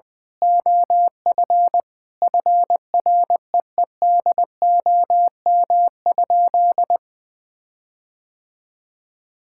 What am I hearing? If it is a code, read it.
Morse code, heard as a keyed tone, about 21 wpm, 700 Hz. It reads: OF FREEDOM?
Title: Sentences from Top 1600 Words 20wpm